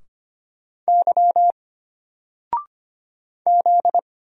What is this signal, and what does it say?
Morse code, heard as a keyed tone, about 25 wpm, 700 Hz. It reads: Y Z
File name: Single-Letter-Number-25wpm